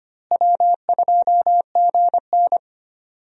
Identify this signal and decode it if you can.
Morse code, heard as a keyed tone, about 25 wpm, 700 Hz. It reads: W2GN
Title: USA Call Signs - 4 Characters - Repeated 2x - 25wpm